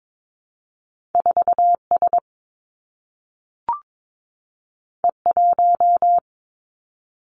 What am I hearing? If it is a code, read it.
Morse code, heard as a keyed tone, about 22 wpm, 700 Hz. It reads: 4 S E 1
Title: Two Challenging Letter-Number Combinations 22wpm